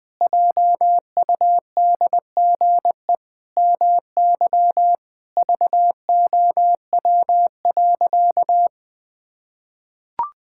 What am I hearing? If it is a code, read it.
Morse code, heard as a keyed tone, about 20 wpm, 700 Hz. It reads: JUDGE MY VOW.
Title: Pangram Challenge v2 - 20wpm